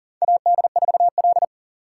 Morse code, heard as a keyed tone, about 40 wpm, 700 Hz. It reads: AD4L
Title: USA Call Signs - 4 Characters - 4x Character Spacing - 40wpm